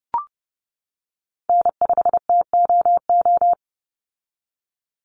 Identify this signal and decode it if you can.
Morse code, heard as a keyed tone, about 30 wpm, 700 Hz. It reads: N5TOO
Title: Call Signs - 30wpm